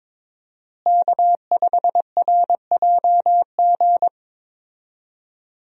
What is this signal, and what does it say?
Morse code, heard as a keyed tone, about 22 wpm, 700 Hz. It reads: K5RJG
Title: USA Call Signs - 5 Characters - Repeated 3x - 22wpm